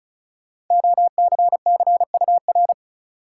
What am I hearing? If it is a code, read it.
Morse code, heard as a keyed tone, about 34 wpm, 700 Hz. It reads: OCCUR
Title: Sentences from Top 900 Words - New Word Drill - 35wpm